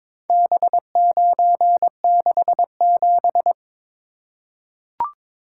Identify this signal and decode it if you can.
Morse code, heard as a keyed tone, about 22 wpm, 700 Hz. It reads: B967
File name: practice / 22wpm / Four Letter-Number Combinations - 22wpm